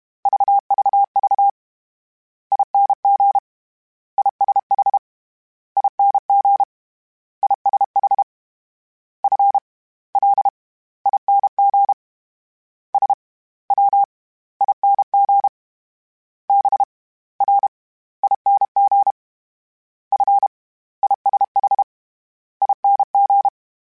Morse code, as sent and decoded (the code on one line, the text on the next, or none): ...- ...- ...- / .. -. --. / .. ... .... / .. -. --. / .. ... .... / ..-. / .-.. / .. -. --. / ... / .-- / .. -. --. / -... / .-. / .. -. --. / ..-. / .. ... .... / .. -. --.
VVV ING ISH ING ISH F L ING S W ING B R ING F ISH ING